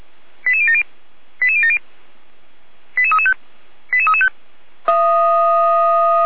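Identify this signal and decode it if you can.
Five-tone selective call (ZVEI), heard as quick runs of five tones, tones of about 70 ms. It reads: 88070 88070 88335 88335
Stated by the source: Five-tone ZVEI selective call received by SDR.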